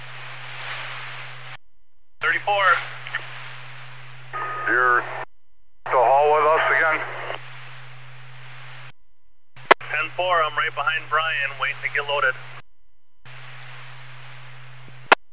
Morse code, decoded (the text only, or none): none